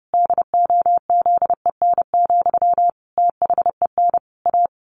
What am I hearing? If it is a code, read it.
Morse code, heard as a keyed tone, about 30 wpm, 700 Hz. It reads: DOZEN, THEN A
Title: A Sound Of Thunder 30wpm